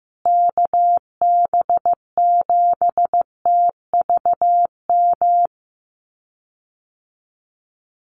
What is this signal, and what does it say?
Morse code, heard as a keyed tone, about 15 wpm, 700 Hz. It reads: KB7TVM
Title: USA Call Signs Repeated 3x 15wpm